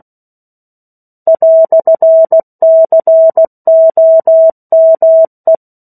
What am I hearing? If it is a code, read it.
Morse code, heard as a keyed tone, about 16 wpm, 650 Hz. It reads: "COME
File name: 03-Chapter-03 16 WPM